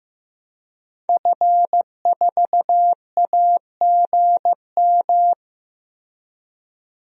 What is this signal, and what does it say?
Morse code, heard as a keyed tone, about 15 wpm, 700 Hz. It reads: F4AGM
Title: DXCC-100 Call Signs - 5 Characters - Repeated 3x - 15wpm